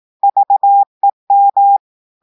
Morse code, heard as a keyed tone, about 18 wpm, 800 Hz. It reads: VEM
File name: audios / portugais / portugais texte en clair 1 vitesse 18 mots minutes